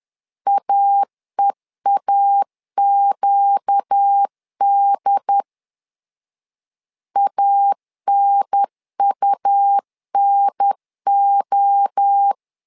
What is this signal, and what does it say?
Morse code, heard as a keyed tone, about 10 wpm, 800 Hz. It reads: AEAQD ANUNO
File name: LEZIONI / 4 Q 10wpm 2210